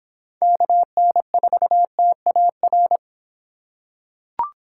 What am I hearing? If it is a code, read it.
Morse code, heard as a keyed tone, about 25 wpm, 700 Hz. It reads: KN4TAR